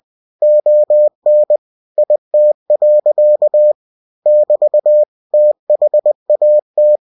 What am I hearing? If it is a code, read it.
Morse code, heard as a keyed tone, about 20 wpm, 600 Hz. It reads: ON IT. <BT> THAT
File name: cw-books / winnie-the-pooh / winnie-20wpm-page-0051